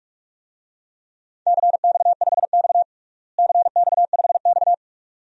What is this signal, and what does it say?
Morse code, heard as a keyed tone, about 47 wpm, 700 Hz. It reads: CX5X CX5X
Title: DXCC-100 Call Signs - 4 Characters - Repeated 2x - 45wpm